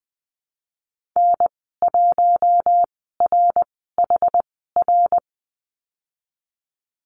Morse code, heard as a keyed tone, about 20 wpm, spaced as for 15 wpm, 700 Hz. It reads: N1RHR
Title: 147 Lesson 24g - Letter y - Call Signs 20wpm